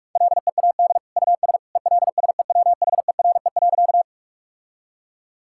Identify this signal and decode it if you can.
Morse code, heard as a keyed tone, about 46 wpm, 700 Hz. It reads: LEAD US ELSEWHERE.
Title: Sentences from Top 1600 Words 45wpm